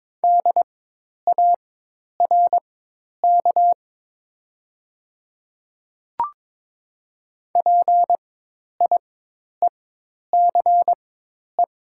Morse code, heard as a keyed tone, about 23 wpm, spaced as for 10 wpm, 700 Hz. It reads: DARK PIECE
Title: 4x Character Spacing - Top 500 Words - 22wpm